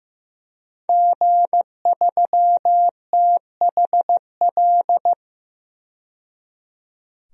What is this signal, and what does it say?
Morse code, heard as a keyed tone, about 15 wpm, 700 Hz. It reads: G3THL
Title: DXCC-100 Call Signs - 5 Characters - 15wpm